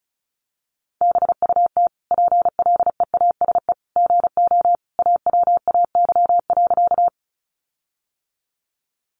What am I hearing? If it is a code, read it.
Morse code, heard as a keyed tone, about 34 wpm, 700 Hz. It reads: BUT PLEASE GO AWAY.
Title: Sentences from Top 1500 Words 35wpm